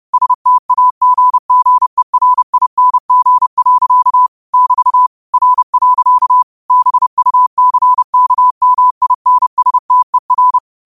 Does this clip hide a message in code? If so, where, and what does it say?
Yes: Morse code from 0.1 to 10.6 s, heard as a keyed tone, about 30 wpm, 1000 Hz. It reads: STAGGERING. <BT> R. BUCKMINSTER